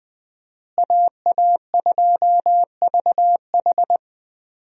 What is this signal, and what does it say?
Morse code, heard as a keyed tone, about 20 wpm, 700 Hz. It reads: AA2VH
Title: USA Call Signs - 5 Characters - Repeated 3x - 20wpm